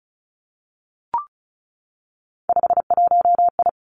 Morse code, heard as a keyed tone, about 36 wpm, 700 Hz. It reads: 51I
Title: Three Challenging Letter-Number Combinations 35wpm